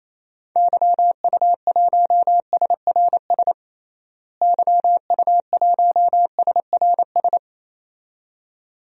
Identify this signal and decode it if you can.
Morse code, heard as a keyed tone, about 28 wpm, 700 Hz. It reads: YU1SRS YU1SRS
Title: DXCC-100 Call Signs - 6 Characters - Repeated 2x - 28wpm